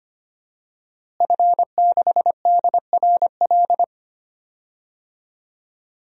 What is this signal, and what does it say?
Morse code, heard as a keyed tone, about 25 wpm, 700 Hz. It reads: F6DRL
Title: DXCC-100 Call Signs - 5 Characters - Repeated 3x - 25wpm